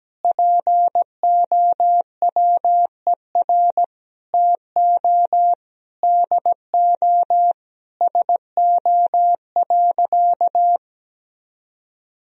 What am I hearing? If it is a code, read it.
Morse code, heard as a keyed tone, about 17 wpm, 700 Hz. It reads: POWER TO DO SO.
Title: Sentences from Top 1000 Words - Review Part 4 - 17wpm